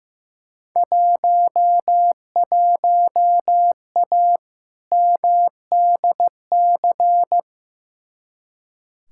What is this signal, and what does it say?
Morse code, heard as a keyed tone, about 15 wpm, 700 Hz. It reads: 11A MDC
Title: Field Day Exchanges Encore - 15wpm